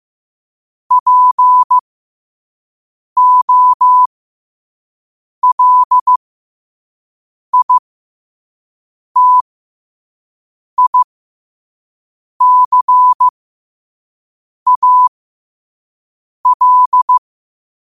Morse code, heard as a keyed tone, about 15 wpm, spaced as for 5 wpm, 1000 Hz. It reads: POLITICAL
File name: qotdfiles / 5wpm